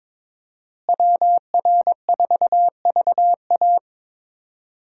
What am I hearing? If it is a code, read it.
Morse code, heard as a keyed tone, about 22 wpm, 700 Hz. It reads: WR4VA
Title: USA Call Signs - 5 Characters - Repeated 2x - 22wpm